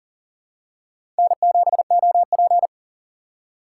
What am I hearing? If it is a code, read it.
Morse code, heard as a keyed tone, about 40 wpm, 700 Hz. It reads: N7OP
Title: USA Call Signs - 4 Characters - 3x Character Spacing - 40wpm